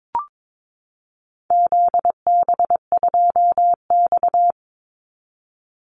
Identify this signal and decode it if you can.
Morse code, heard as a keyed tone, about 22 wpm, 700 Hz. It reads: ZB2X
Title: Call Signs Repeated 2x - 22wpm